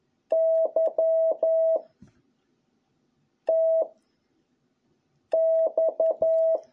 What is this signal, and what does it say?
Morse code, heard as a keyed tone, about 11 wpm, 650 Hz. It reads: Y T X